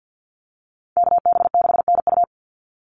Morse code, heard as a keyed tone, about 48 wpm, 700 Hz. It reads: KB6NU
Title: Call Signs Repeated 3x - 50wpm